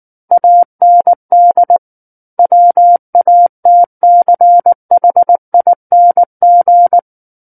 Morse code, heard as a keyed tone, about 19 wpm, 700 Hz. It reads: AND WATCHING
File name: Book One / 10-Chapter 10 - 19 WPM